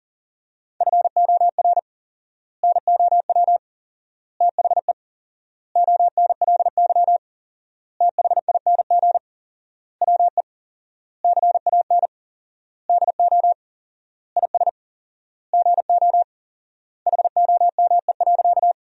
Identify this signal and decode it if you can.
Morse code, heard as a keyed tone, about 40 wpm, 700 Hz. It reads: FOR NOW THE ONLY THING WE CAN DO IS GO HOME.
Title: 4x Word-Spacing - Sentences from Top 200 Words - 40wpm